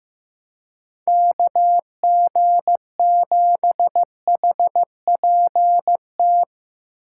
Morse code, heard as a keyed tone, about 15 wpm, 700 Hz. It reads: KG7HPT
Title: USA Call Signs - 6 Characters - 4x Character Spacing - 15wpm